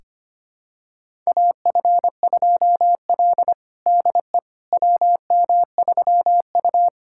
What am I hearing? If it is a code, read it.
Morse code, heard as a keyed tone, about 25 wpm, 700 Hz. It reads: AF2L DE WM3U
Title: USA Callsign DE Callsign - 4 Characters - 25wpm